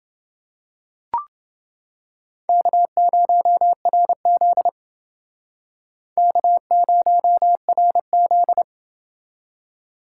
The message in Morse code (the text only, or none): K0RZ K0RZ